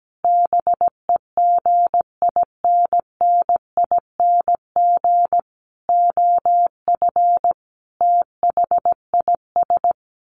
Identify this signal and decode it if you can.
Morse code, heard as a keyed tone, about 17 wpm, 700 Hz. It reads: BEGINNING OF THIS
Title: Sets of 5 Words Encore 17wpm